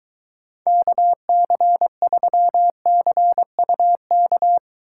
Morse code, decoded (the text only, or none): KC3CUK